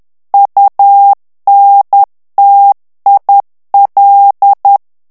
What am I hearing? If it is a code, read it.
Morse code, heard as a keyed tone, about 10 wpm, 800 Hz. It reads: UNTIL
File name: ANGLAIS texte clair 3 vitesse 10 mots minute